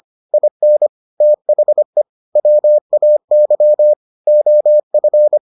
Morse code, heard as a keyed tone, about 25 wpm, 600 Hz. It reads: IN THE WAY OF